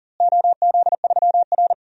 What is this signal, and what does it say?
Morse code, heard as a keyed tone, about 40 wpm, 700 Hz. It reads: OZ3R
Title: DXCC-100 Call Signs - 4 Characters - Repeated 2x - 40wpm